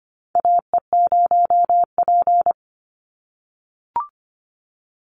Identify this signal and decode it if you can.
Morse code, heard as a keyed tone, about 25 wpm, 700 Hz. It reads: AE0P
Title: Call Signs - 25wpm